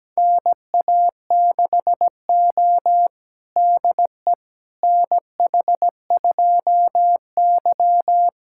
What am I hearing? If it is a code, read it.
Morse code, heard as a keyed tone, about 17 wpm, 700 Hz. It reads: NA6O DE NH2Y
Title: USA Callsign DE Callsign - 4 Characters - 17wpm